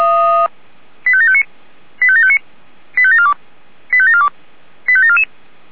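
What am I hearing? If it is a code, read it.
Five-tone selective call (ZVEI), heard as quick runs of five tones, tones of about 70 ms. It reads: 85759 85759 85742 85742 85744
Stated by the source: ZVEI five-tone selective call 85729 received on SDR.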